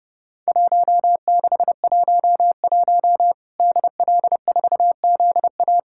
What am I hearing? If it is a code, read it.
Morse code, heard as a keyed tone, about 28 wpm, 700 Hz. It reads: 1611 DL4ZA
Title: Worked All Europe DX Contest - QTC Exchange - 30wpm